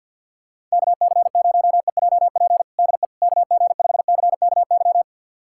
Morse code, heard as a keyed tone, about 51 wpm, 700 Hz. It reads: KC0EJP DE KG5CKY
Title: USA Callsign DE Callsign - 6 Characters - 50wpm